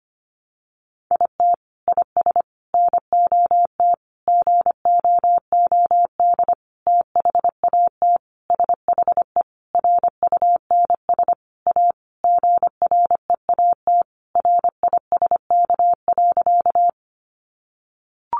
Morse code, heard as a keyed tone, about 25 wpm, 700 Hz. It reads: IT IS NOT GOOD THAT SHE RUNS A GREAT RISK.
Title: Sentences from Top 600 Words 25wpm